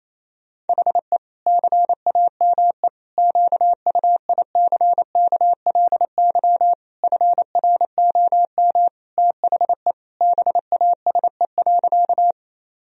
Morse code, heard as a keyed tone, about 28 wpm, 700 Hz. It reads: HE CAME QUICKLY FROM THE BASE.
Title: Sentences from Top 1000 Words - Review Part 1 - 28wpm